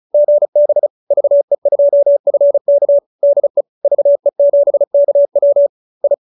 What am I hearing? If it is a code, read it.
Morse code, heard as a keyed tone, about 34 wpm, 570 Hz. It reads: GB VE2FK DE VE7KW I
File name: QSO314 35